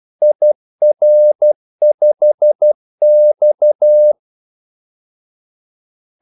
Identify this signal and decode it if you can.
Morse code, heard as a keyed tone, about 12 wpm, 600 Hz. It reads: IR5X
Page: 2 ekstra ordmellemrum mellem kaldesignalerne